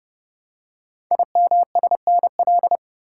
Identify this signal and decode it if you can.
Morse code, heard as a keyed tone, about 30 wpm, 700 Hz. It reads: IMSNL
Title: Code Groups - 5 Letters - 30wpm